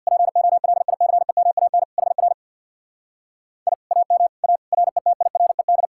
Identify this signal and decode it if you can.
Morse code, heard as a keyed tone, about 64 wpm, 700 Hz. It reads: POLICEMAN HR I AM A RETIRED